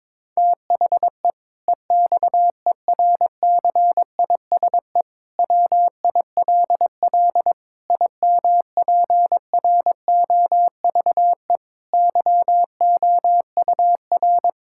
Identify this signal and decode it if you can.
Morse code, heard as a keyed tone, about 22 wpm, 700 Hz. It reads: THE EXERCISE WILL IMPROVE YOUR
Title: Sentences from Top 1000 Words - Review Part 6 - 22wpm